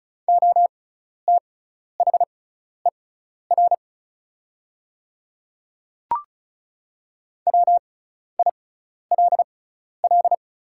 Morse code, heard as a keyed tone, about 36 wpm, spaced as for 12 wpm, 690 Hz. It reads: OTHER WILL
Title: 6x Character Spacing - Top 300 Words - 35wpm